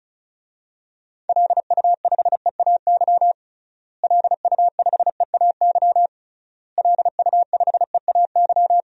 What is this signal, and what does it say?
Morse code, heard as a keyed tone, about 34 wpm, 700 Hz. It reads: LU5EAY LU5EAY LU5EAY
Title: DXCC-100 Call Signs - 6 Characters - Repeated 3x - 35wpm